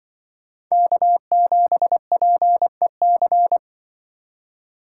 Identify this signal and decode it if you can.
Morse code, heard as a keyed tone, about 24 wpm, 700 Hz. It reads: K7PEC
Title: Speed Racing - USA 1x3 Call Signs 20wpm